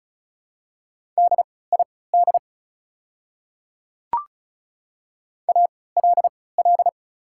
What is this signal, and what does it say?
Morse code, heard as a keyed tone, about 34 wpm, spaced as for 20 wpm, 690 Hz. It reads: DID ALL
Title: 3x Character Spacing - Top 100 Words - 35wpm